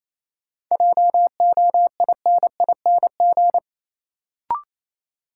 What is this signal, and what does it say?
Morse code, heard as a keyed tone, about 28 wpm, 700 Hz. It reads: JOINING
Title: Sets of 5 Words - New Word Drill - 28wpm